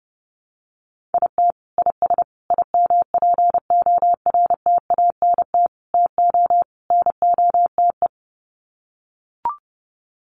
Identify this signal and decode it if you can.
Morse code, heard as a keyed tone, about 30 wpm, 700 Hz. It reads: IT IS IMPORTANT TO NOTE
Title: Sets of 5 Words 30wpm